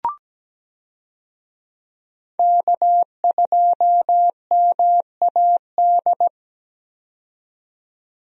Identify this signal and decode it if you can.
Morse code, heard as a keyed tone, about 17 wpm, 700 Hz. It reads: K2MAD
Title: USA Call Signs - 5 Characters - 17wpm